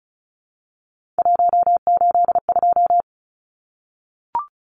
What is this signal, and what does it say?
Morse code, heard as a keyed tone, about 34 wpm, 700 Hz. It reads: 182
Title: Numbers - 3-Digit 35wpm